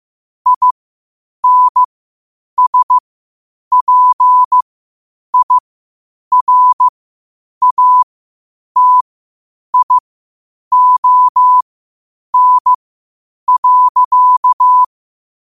Morse code, heard as a keyed tone, about 15 wpm, spaced as for 8 wpm, 1000 Hz. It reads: INSPIRATION.
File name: qotdfiles / 8wpm